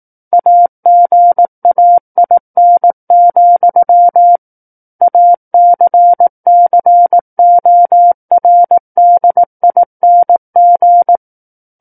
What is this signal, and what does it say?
Morse code, heard as a keyed tone, about 18 wpm, 700 Hz. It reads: AGAIN, ACCORDING
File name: 18 WPM